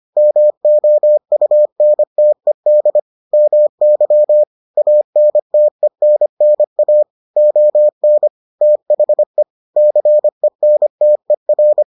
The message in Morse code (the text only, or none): MOUNTED MY ANTENNA ON THE CENTER